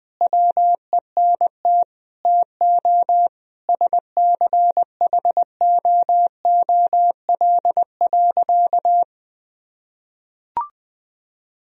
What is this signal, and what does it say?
Morse code, heard as a keyed tone, about 21 wpm, 700 Hz. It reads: WENT TO SCHOOL.
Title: Sentences from Top 1000 Words - Review Part 1 - 20wpm